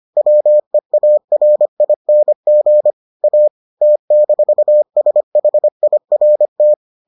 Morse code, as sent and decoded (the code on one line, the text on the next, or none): .-- . .- .-. .. -. --. / .- / - -....- ... .... .. .-. -
WEARING A T-SHIRT